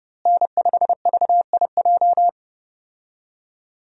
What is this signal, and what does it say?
Morse code, heard as a keyed tone, about 30 wpm, 700 Hz. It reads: N5VIJ
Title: USA Call Signs - 5 Characters - Repeated 2x - 30wpm